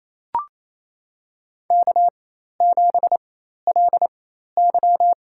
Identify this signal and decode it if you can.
Morse code, heard as a keyed tone, about 28 wpm, spaced as for 13 wpm, 700 Hz. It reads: K7LY
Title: USA Call Signs - 4 Characters - 4x Character Spacing - 28wpm